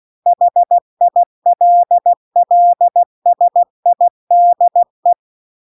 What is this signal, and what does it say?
Morse code, heard as a keyed tone, about 16 wpm, 700 Hz. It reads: HILLSIDE